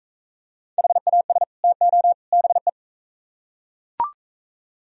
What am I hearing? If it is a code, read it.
Morse code, heard as a keyed tone, about 40 wpm, 700 Hz. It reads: HAS TO BE